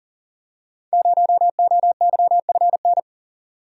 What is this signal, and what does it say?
Morse code, heard as a keyed tone, about 43 wpm, 700 Hz. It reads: 0OYFN
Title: Code Groups - 5 Letter:Numbers - 40wpm